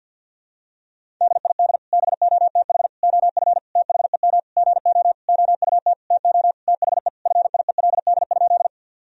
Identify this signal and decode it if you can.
Morse code, heard as a keyed tone, about 52 wpm, 700 Hz. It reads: DID BOTH OF THEM GO OUT TO THE FIELD?
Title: Sentences from Top 1000 Words - Review Part 2 - 50wpm